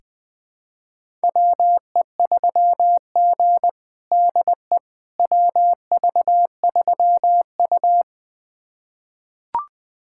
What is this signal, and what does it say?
Morse code, heard as a keyed tone, about 20 wpm, 700 Hz. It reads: WE3G DE WV3U
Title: USA Callsign DE Callsign - 4 Characters - 20wpm